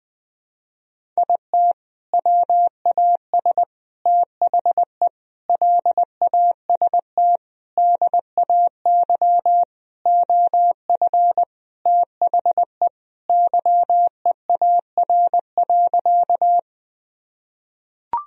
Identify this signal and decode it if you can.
Morse code, heard as a keyed tone, about 21 wpm, 700 Hz. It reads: IT WAS THE LAST DAY OF THE YEAR.
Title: Sentences from Top 1000 Words - Review Part 1 - 20wpm